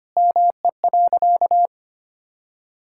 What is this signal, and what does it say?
Morse code, heard as a keyed tone, about 25 wpm, 700 Hz. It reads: ME.
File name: practice / 25wpm / Sentences with 4-Word Phrases - 25wpm